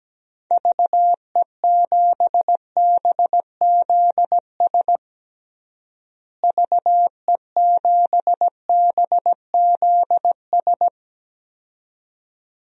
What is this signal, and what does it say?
Morse code, heard as a keyed tone, about 17 wpm, 700 Hz. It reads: VE7BZS VE7BZS
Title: DXCC-100 Call Signs - 6 Characters - Repeated 2x - 17wpm